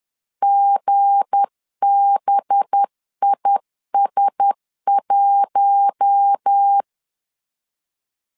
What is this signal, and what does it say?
Morse code, heard as a keyed tone, about 10 wpm, 800 Hz. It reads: GBIS1
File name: LEZIONI / 7 GF9Z / 7 GF 10wpm 2514